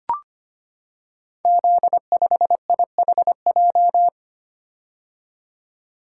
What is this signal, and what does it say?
Morse code, heard as a keyed tone, about 25 wpm, 700 Hz. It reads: Z5IHJ
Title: Code Groups - 5 Letter:Numbers - 25wpm